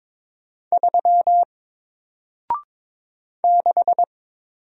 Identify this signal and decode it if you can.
Morse code, heard as a keyed tone, about 22 wpm, 700 Hz. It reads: 3 6
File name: Single-Letter-Number-22wpm